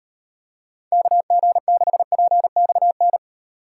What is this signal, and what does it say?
Morse code, heard as a keyed tone, about 38 wpm, 700 Hz. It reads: KG6PXN